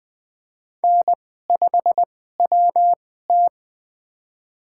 Morse code, heard as a keyed tone, about 20 wpm, spaced as for 14 wpm, 700 Hz. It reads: N5WT
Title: USA Call Signs - 4 Characters - 2x Character Spacing - 20wpm